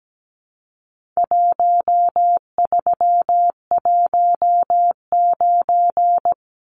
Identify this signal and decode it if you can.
Morse code, heard as a keyed tone, about 17 wpm, 700 Hz. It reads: 1319
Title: Numbers - Contest Style - 4-Digit 17wpm